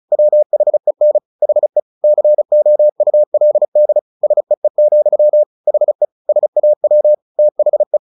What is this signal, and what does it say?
Morse code, heard as a keyed tone, about 34 wpm, 600 Hz. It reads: WHEN HE COULD SEE, HE SAW THE
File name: the old man and the sea 2 35wpm